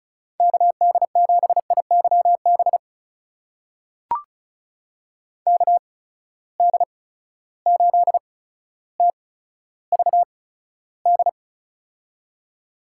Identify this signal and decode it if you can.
Morse code, heard as a keyed tone, about 36 wpm, spaced as for 15 wpm, 700 Hz. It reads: KD7IYB KD8TVD
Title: USA Call Signs - 6 Characters - 8x Character Spacing - 35wpm